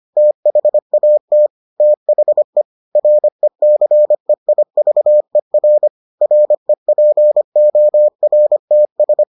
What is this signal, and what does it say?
Morse code, heard as a keyed tone, about 25 wpm, 600 Hz. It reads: THAT THE RECEIVER REPORTS